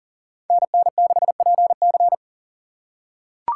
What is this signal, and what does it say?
Morse code, heard as a keyed tone, about 40 wpm, 700 Hz. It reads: NN6PC
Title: USA Call Signs - 5 Characters - 6x Character Spacing - 40wpm